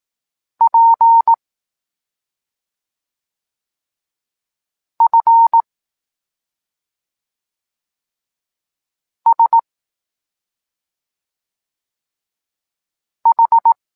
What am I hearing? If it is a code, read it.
Morse code, heard as a keyed tone, about 18 wpm, 920 Hz. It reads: P F S H